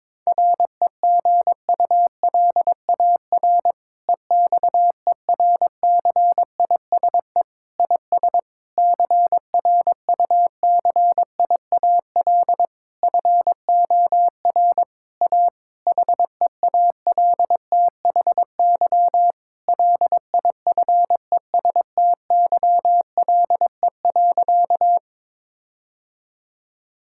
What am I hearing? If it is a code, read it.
Morse code, heard as a keyed tone, about 22 wpm, 700 Hz. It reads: REGULAR EXERCISE IS CRUCIAL FOR A HEALTHY LIFESTYLE.
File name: Difficult Sentences with Hints - 22wpm